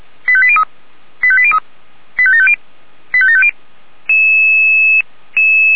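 Five-tone selective call (ZVEI), heard as quick runs of five tones, tones of about 70 ms. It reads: 85803 85803 85750 85750
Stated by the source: ZVEI five-tone selective call received by SDR.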